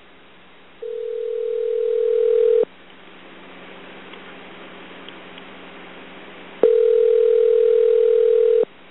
Ringback tone (North America), 440 + 480 Hz.